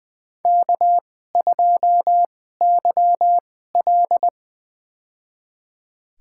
Morse code, heard as a keyed tone, about 21 wpm, spaced as for 15 wpm, 700 Hz. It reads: K2YL
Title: USA Call Signs - 4 Characters - 2x Character Spacing - 20wpm